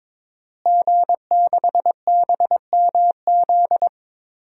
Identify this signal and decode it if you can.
Morse code, heard as a keyed tone, about 22 wpm, 700 Hz. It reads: G6BMZ
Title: DXCC-100 Call Signs - 5 Characters - Repeated 3x - 22wpm